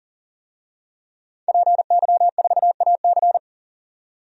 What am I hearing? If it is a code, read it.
Morse code, heard as a keyed tone, about 40 wpm, 700 Hz. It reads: PY4AC
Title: DXCC-100 Call Signs - 5 Characters - 40wpm